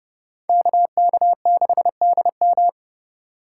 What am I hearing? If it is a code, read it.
Morse code, heard as a keyed tone, about 31 wpm, 700 Hz. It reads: KK6DM
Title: USA Call Signs - 4x Character Spacing - 30wpm